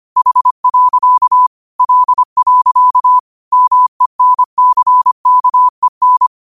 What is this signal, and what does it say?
Morse code, heard as a keyed tone, about 25 wpm, 1000 Hz. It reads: H. L. MENCKEN
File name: qotdfiles / 25wpm